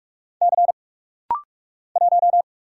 Morse code, heard as a keyed tone, about 46 wpm, 700 Hz. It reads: C 1
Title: Single Letter-Number Alt - Rapid-Fire 45wpm